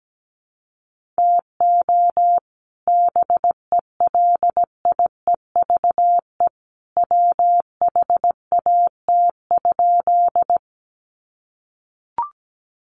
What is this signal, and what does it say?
Morse code, heard as a keyed tone, about 17 wpm, 700 Hz. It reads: TO BELIEVE WHAT?
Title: Sentences from Top 1000 Words 17wpm